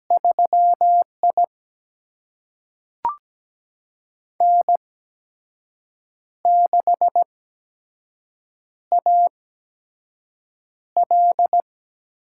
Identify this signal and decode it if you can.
Morse code, heard as a keyed tone, about 17 wpm, spaced as for 6 wpm, 700 Hz. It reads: 3I N6AL
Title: USA Call Signs - 4 Characters - 8x Character Spacing - 17wpm